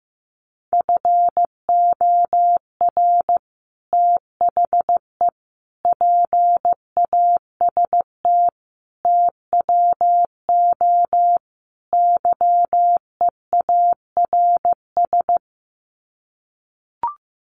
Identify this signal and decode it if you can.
Morse code, heard as a keyed tone, about 15 wpm, 700 Hz. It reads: FOR THE PAST TWO YEARS
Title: Sets of 5 Words Encore2 15wpm